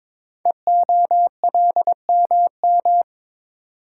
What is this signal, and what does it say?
Morse code, heard as a keyed tone, about 22 wpm, 690 Hz. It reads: EOLMM
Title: Code Groups - 5 Letters - 22wpm